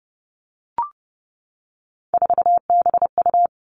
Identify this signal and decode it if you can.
Morse code, heard as a keyed tone, about 30 wpm, 700 Hz. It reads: 4BU
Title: Three Challenging Letter-Number Combinations 30wpm